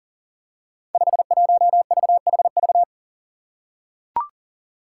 Morse code, heard as a keyed tone, about 40 wpm, 700 Hz. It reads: 51VHV
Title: Five Challenging Letter-Number Combinations 40wpm